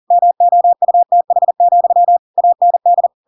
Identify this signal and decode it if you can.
Morse code, heard as a keyed tone, about 40 wpm, 700 Hz. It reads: MOUTH, AND
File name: Pinocchio / Pinocchio ch5 40 40 ik1hge